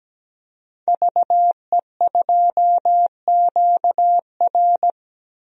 Morse code, heard as a keyed tone, about 17 wpm, 700 Hz. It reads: VE2QR
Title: DXCC-100 Call Signs - 5 Characters - Repeated 3x - 17wpm